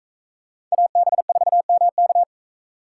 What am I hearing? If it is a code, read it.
Morse code, heard as a keyed tone, about 40 wpm, 700 Hz. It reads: AB4MK